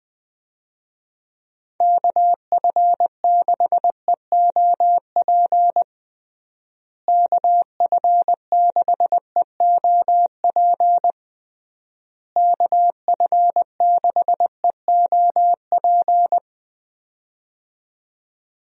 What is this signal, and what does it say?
Morse code, heard as a keyed tone, about 21 wpm, 700 Hz. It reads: KF6EOP KF6EOP KF6EOP
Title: USA Call Signs - 6 Characters - Repeated 3x - 20wpm